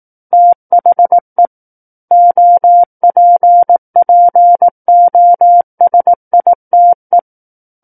Morse code, heard as a keyed tone, about 18 wpm, 700 Hz. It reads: THE OPPOSITE
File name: 18 WPM